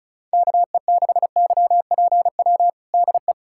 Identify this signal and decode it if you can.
Morse code, heard as a keyed tone, about 36 wpm, 690 Hz. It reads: KE6YPW DE
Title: USA Callsign DE Callsign - 6 Characters - 35wpm